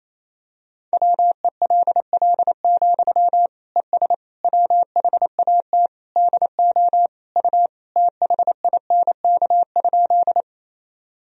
Morse code, heard as a keyed tone, about 28 wpm, 700 Hz. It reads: WELL, ES WHAT DO U THINK?
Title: Sentences from Top 200 Words and Abbreviations 28wpm